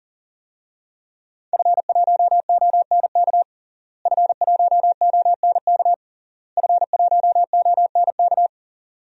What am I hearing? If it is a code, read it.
Morse code, heard as a keyed tone, about 40 wpm, 700 Hz. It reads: F1ONK F1ONK F1ONK
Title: DXCC-100 Call Signs - 5 Characters - Repeated 3x - 40wpm